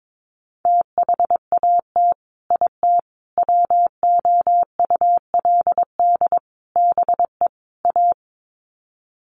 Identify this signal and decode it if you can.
Morse code, heard as a keyed tone, about 22 wpm, 700 Hz. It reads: THAT IT WOULD BE A
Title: Sets of 5 Words Encore2 22wpm